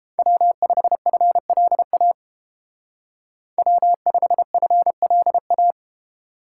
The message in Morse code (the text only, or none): W5FLA W5FLA